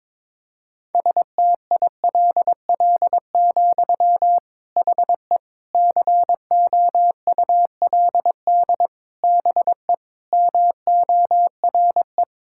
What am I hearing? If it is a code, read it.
Morse code, heard as a keyed tone, about 22 wpm, 700 Hz. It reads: STILL, HE COULD BE MORE
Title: Sentences from Top 1000 Words - Review Part 5 - 22wpm